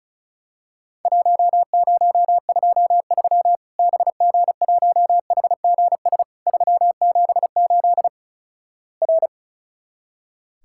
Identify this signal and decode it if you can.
Morse code, heard as a keyed tone, about 36 wpm, 700 Hz. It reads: 1023 BG1HGS 378 R
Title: Worked All Europe DX Contest - QTC Exchange - 35wpm